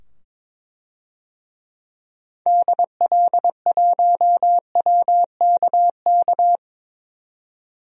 Morse code, heard as a keyed tone, about 22 wpm, 700 Hz. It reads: DL1WKK
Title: DXCC-100 Call Signs - 6 Characters - 22wpm